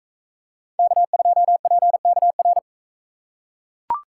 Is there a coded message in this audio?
Yes: Morse code from 0.8 to 2.6 s, heard as a keyed tone, about 42 wpm, 700 Hz. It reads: K2PKR